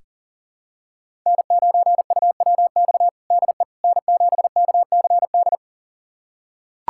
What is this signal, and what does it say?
Morse code, heard as a keyed tone, about 37 wpm, 700 Hz. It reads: N9UWX DE N7KCD
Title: USA Callsign DE Callsign - 5 Characters - 40wpm